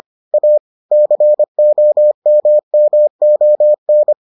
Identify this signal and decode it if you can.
Morse code, heard as a keyed tone, about 25 wpm, 600 Hz. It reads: A COMMON